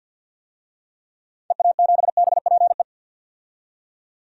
Morse code, heard as a keyed tone, about 52 wpm, 700 Hz. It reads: EA7BPE
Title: DXCC-100 Call Signs - 6 Characters - 50wpm